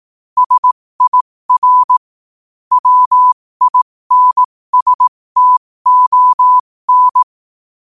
Morse code, heard as a keyed tone, about 18 wpm, spaced as for 15 wpm, 1000 Hz. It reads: SIR WINSTON
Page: 15 wpm morse code quotes